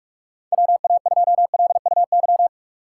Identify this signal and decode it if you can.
Morse code, heard as a keyed tone, about 41 wpm, 700 Hz. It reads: WA2LUY
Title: USA Call Signs - 6 Characters - 2x Character Spacing - 45wpm